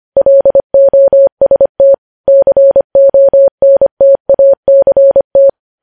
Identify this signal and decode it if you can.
Morse code, heard as a keyed tone, about 25 wpm, 550 Hz. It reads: LOST CONTACT
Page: News Headlines in Morse Code at 25 WPM